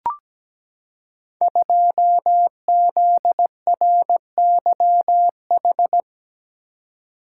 Morse code, heard as a keyed tone, about 17 wpm, 700 Hz. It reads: 2ZRYH
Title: Code Groups - 5 Letter:Numbers - 17wpm